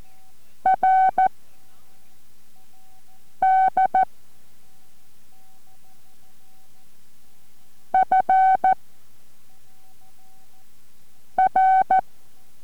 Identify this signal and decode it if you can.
Morse code, heard as a keyed tone, about 14 wpm, 760 Hz. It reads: R D F R